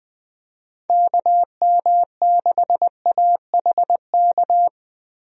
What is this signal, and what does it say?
Morse code, heard as a keyed tone, about 21 wpm, 700 Hz. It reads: KM6AHK
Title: USA Call Signs - 6 Characters - Repeated 3x - 20wpm